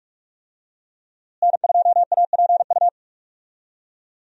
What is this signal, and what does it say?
Morse code, heard as a keyed tone, about 42 wpm, 700 Hz. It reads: N2APU
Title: USA Call Signs - 5 Characters - Repeated 2x - 45wpm